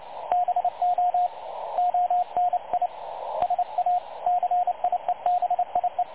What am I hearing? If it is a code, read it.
Morse code, heard as a keyed tone, about 29 wpm, 720 Hz, with radio noise. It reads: BO ONI SA CIEBIE
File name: dlazaawansowanych / nagraniazpasma / websdr recording 3565.0kHz